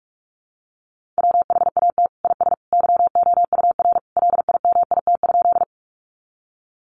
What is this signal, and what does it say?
Morse code, heard as a keyed tone, about 44 wpm, 700 Hz. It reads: WHAT IS YOUR LIMIT?
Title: Sentences from Top 1500 Words 45wpm